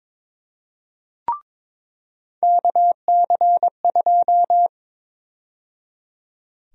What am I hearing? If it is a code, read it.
Morse code, heard as a keyed tone, about 22 wpm, 700 Hz. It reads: KC2
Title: USA Call Sign Prefixes - 22wpm